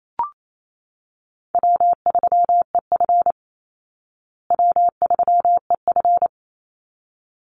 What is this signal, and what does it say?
Morse code, heard as a keyed tone, about 28 wpm, 700 Hz. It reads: W3EF W3EF
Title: Call Signs Repeated 3x - 28wpm